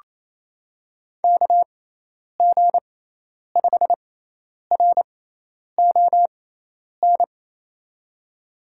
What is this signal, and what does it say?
Morse code, heard as a keyed tone, about 28 wpm, spaced as for 10 wpm, 700 Hz. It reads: KG5RON
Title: USA Call Signs - 6x Character Spacing - 28wpm